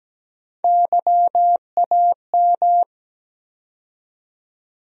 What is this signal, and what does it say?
Morse code, heard as a keyed tone, about 17 wpm, 700 Hz. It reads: YAM
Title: USA Call Sign Postfixes - 3 Letters 17wpm